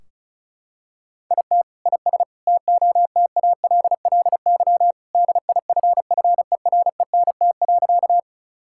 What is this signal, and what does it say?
Morse code, heard as a keyed tone, about 34 wpm, 700 Hz. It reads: IT IS TOTALLY DIFFERENT.
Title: Sentences from Top 1600 Words 35wpm